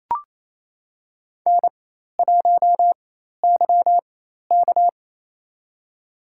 Morse code, heard as a keyed tone, about 28 wpm, spaced as for 13 wpm, 700 Hz. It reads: N1YK
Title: USA Call Signs - 4 Characters - 4x Character Spacing - 28wpm